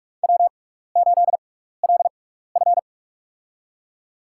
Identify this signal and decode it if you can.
Morse code, heard as a keyed tone, about 47 wpm, spaced as for 16 wpm, 700 Hz. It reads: W8LF
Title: USA Call Signs - 4 Characters - 6x Character Spacing - 45wpm